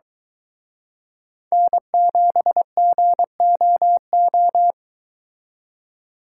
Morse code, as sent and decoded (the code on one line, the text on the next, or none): -. --... --. --- ---
N7GOO